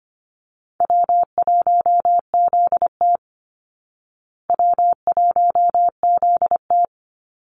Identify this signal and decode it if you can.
Morse code, heard as a keyed tone, about 25 wpm, 700 Hz. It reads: W1ZT W1ZT
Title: Call Signs Repeated 2x - 25wpm